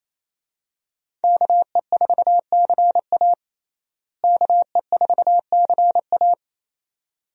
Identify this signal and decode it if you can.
Morse code, heard as a keyed tone, about 27 wpm, 700 Hz. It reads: KE4CA KE4CA
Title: USA Call Signs - 5 Characters - Repeated 2x - 28wpm